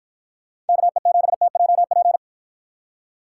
Morse code, heard as a keyed tone, about 50 wpm, 700 Hz. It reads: KE7TJP